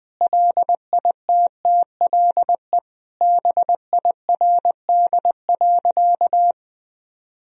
Morse code, heard as a keyed tone, about 21 wpm, 700 Hz. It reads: LITTLE BIRD.
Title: Sentences from Top 1000 Words - Review Part 2 - 20wpm